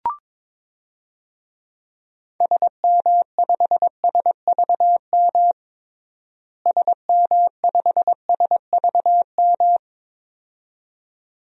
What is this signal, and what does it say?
Morse code, heard as a keyed tone, about 22 wpm, 700 Hz. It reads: SM5SVM SM5SVM
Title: DXCC-100 Call Signs - 6 Characters - Repeated 2x - 22wpm